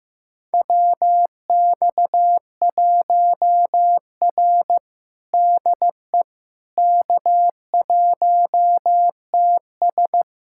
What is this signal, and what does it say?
Morse code, heard as a keyed tone, about 15 wpm, 700 Hz. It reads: WX1R DE K1TS
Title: USA Callsign DE Callsign - 4 Characters - 15wpm